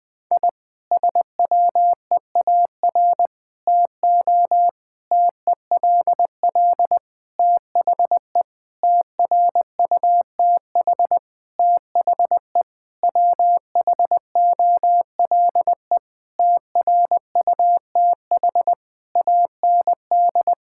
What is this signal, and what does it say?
Morse code, heard as a keyed tone, about 20 wpm, 700 Hz. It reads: I SWEAR TO TELL THE TRUTH THE WHOLE TRUTH AND
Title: Sentences with 9-Word Phrases - 20wpm